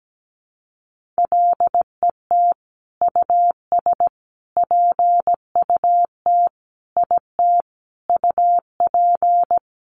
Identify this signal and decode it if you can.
Morse code, heard as a keyed tone, about 17 wpm, 700 Hz. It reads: LET US PUT IT UP
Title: Sentences from Top 1400 Words 17wpm